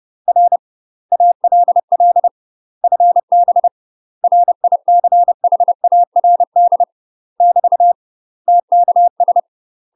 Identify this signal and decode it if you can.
Morse code, heard as a keyed tone, about 30 wpm, 700 Hz. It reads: R ALL FB RICHARD <BT> TKS
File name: UK MASTER QSOs TRAINING/30WPM